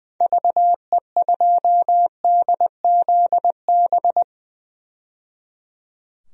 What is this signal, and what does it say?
Morse code, heard as a keyed tone, about 20 wpm, 700 Hz. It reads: VE2DZB
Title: DXCC-100 Call Signs - 6 Characters - 20wpm